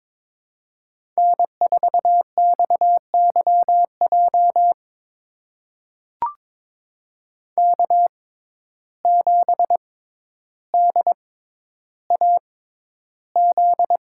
Morse code, heard as a keyed tone, about 22 wpm, spaced as for 11 wpm, 700 Hz. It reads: N4XYJ K7DAZ
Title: USA Call Signs - 5 Characters - 6x Character Spacing - 22wpm